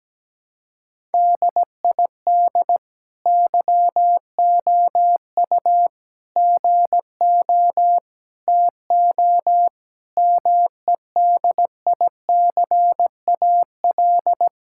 Morse code, heard as a keyed tone, about 17 wpm, 700 Hz. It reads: DID YOU GO TO MEDICAL
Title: Sentences from Top 1000 Words - Review Part 5 - 17wpm